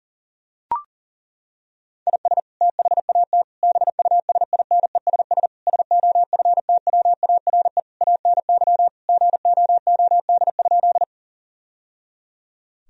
Morse code, heard as a keyed tone, about 37 wpm, 700 Hz. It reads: IS THAT BUSINESS SOFTWARE ANY GOOD?
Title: Sentences from Top 1000 Words - Review Part 3 - 40wpm